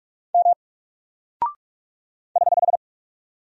Morse code, heard as a keyed tone, about 46 wpm, 700 Hz. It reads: M <HH>
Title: Single Character (Letter:Number:Punctuation:Prosign) Alt 45wpm